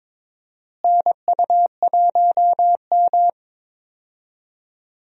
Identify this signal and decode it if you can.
Morse code, heard as a keyed tone, about 22 wpm, 700 Hz. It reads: NU1M
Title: USA Call Signs - 4 Characters - Repeated 3x - 22wpm